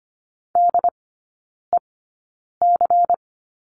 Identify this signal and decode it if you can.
Morse code, heard as a keyed tone, about 25 wpm, spaced as for 9 wpm, 700 Hz. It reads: DEC
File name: course / 25:10wpm / 093 Lesson 16e - Letter c - Three Characters 25wpm